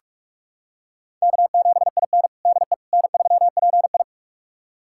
Morse code, heard as a keyed tone, about 43 wpm, 700 Hz. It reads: K7IN DE N3PI
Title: USA Callsign DE Callsign - 4 Characters - 45wpm